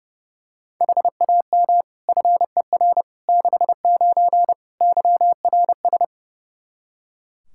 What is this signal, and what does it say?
Morse code, heard as a keyed tone, about 31 wpm, 700 Hz. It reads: HAM FER 69 YRS
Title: K1USN SST QSO Practice 30wpm